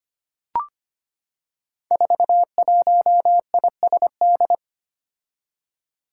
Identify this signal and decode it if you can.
Morse code, heard as a keyed tone, about 25 wpm, 700 Hz. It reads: 41ISD
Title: Five Challenging Letter-Number Combinations 25wpm